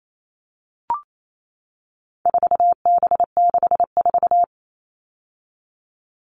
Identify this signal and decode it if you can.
Morse code, heard as a keyed tone, about 28 wpm, 700 Hz. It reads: 4B64
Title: All Cut Numbers - 4-Digit 28wpm